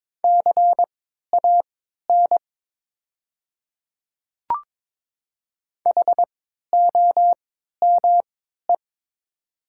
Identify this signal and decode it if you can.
Morse code, heard as a keyed tone, about 22 wpm, spaced as for 12 wpm, 700 Hz. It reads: CAN HOME
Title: 3x Character Spacing - Top 200 Words - 22wpm